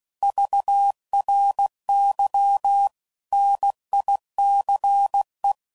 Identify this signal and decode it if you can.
Morse code, heard as a keyed tone, about 16 wpm, 800 Hz. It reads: VRY NICE
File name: CW Code Groups @15WPM / QSOs / WA7QOX QRX